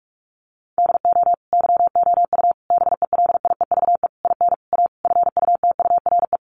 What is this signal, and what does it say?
Morse code, heard as a keyed tone, about 44 wpm, 700 Hz. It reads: DO YOU BELIEVE IN A FUTURE
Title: Sentences from Top 1000 Words 45wpm